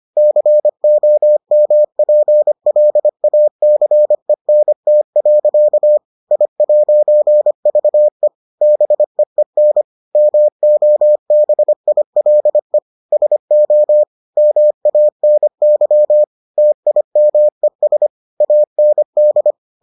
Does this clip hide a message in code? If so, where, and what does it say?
Yes: Morse code from 0.2 to 19.5 s, heard as a keyed tone, about 25 wpm, 600 Hz. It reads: COMPLACENT. I'VE BEEN MOBILE SO MANY TIMES AND